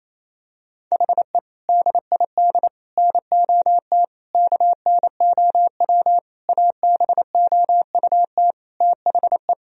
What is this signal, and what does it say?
Morse code, heard as a keyed tone, about 27 wpm, 700 Hz. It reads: HE DID NOT KNOW ABOUT THE
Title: Sentences from Top 1000 Words - Review Part 3 - 28wpm